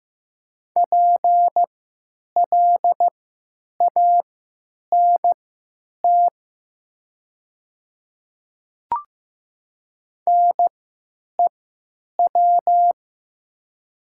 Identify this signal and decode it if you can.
Morse code, heard as a keyed tone, about 15 wpm, spaced as for 9 wpm, 700 Hz. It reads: PLANT NEW
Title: 3x Character Spacing - Top 300 Words - 15wpm